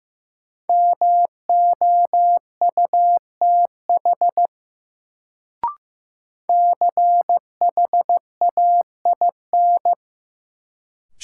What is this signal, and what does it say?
Morse code, heard as a keyed tone, about 15 wpm, 700 Hz. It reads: MOUTH CHAIN
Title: Sentences from Top 1500 Words - New Word Drill - 15wpm